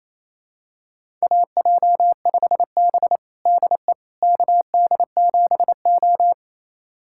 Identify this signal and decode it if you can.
Morse code, heard as a keyed tone, about 28 wpm, 700 Hz. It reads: AJ5B DE KD7O
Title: USA Callsign DE Callsign - 4 Characters - 28wpm